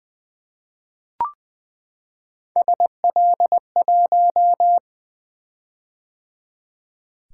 Morse code, heard as a keyed tone, about 20 wpm, 700 Hz. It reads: SL1
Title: Three Letter-Number Combinations - 20wpm